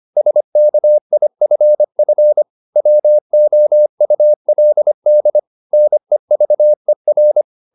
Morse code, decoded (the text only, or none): SKIFF WOULD NEVER